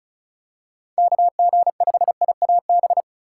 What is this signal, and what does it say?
Morse code, heard as a keyed tone, about 34 wpm, 700 Hz. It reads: KG5IAB
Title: USA Call Signs - 6 Characters - 4x Character Spacing - 35wpm